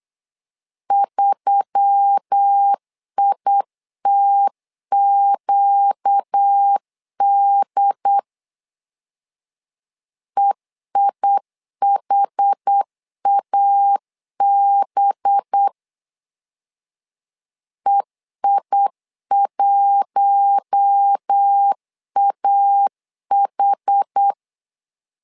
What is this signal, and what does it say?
Morse code, heard as a keyed tone, about 8 wpm, 800 Hz. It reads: 3ITQD EIHAB EI1AH
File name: LEZIONI / 5 LBVC / 5 LBV 8wpm 2837